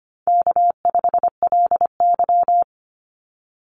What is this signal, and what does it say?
Morse code, heard as a keyed tone, about 25 wpm, 700 Hz. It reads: K5LY
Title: Call Signs - 25wpm